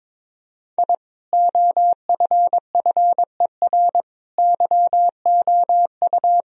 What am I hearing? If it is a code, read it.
Morse code, heard as a keyed tone, about 22 wpm, 700 Hz. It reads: I OFFER YOU
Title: Sentences from Top 1000 Words - Review Part 4 - 22wpm